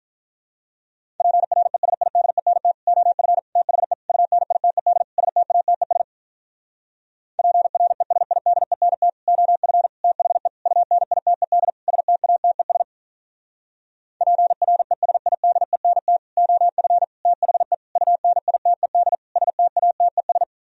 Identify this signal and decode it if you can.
Morse code, heard as a keyed tone, about 46 wpm, 700 Hz. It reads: PRESIDENT OF THE UNITED STATES PRESIDENT OF THE UNITED STATES PRESIDENT OF THE UNITED STATES